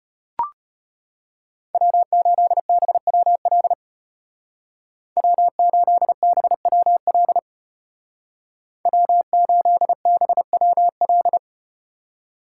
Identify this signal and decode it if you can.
Morse code, heard as a keyed tone, about 34 wpm, 700 Hz. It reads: W8BWL W8BWL W8BWL